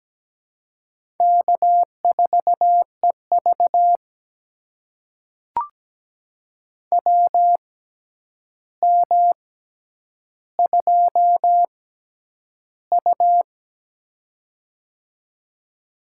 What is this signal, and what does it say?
Morse code, heard as a keyed tone, about 17 wpm, spaced as for 9 wpm, 700 Hz. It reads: K4EV WM2U
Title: USA Call Signs - 4 Characters - 6x Character Spacing - 17wpm